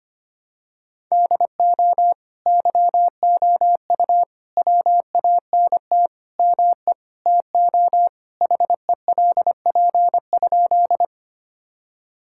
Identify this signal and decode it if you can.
Morse code, heard as a keyed tone, about 25 wpm, 700 Hz. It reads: DO YOU WANT ME TO HELP?
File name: practice / 25wpm / Sentences with 4-Word Phrases - 25wpm